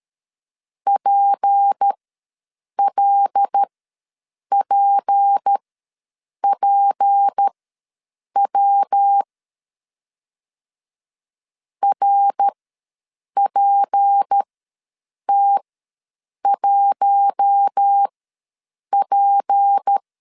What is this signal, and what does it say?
Morse code, heard as a keyed tone, about 13 wpm, spaced as for 7 wpm, 800 Hz. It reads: PLPPW RPT1P
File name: LEZIONI / 6 PKX0 / 6 P ws 12wpm 3117